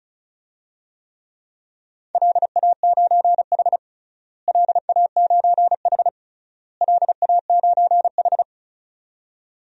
Morse code, heard as a keyed tone, about 34 wpm, 700 Hz. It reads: LA9H LA9H LA9H
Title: DXCC-100 Call Signs - 4 Characters - Repeated 3x - 35wpm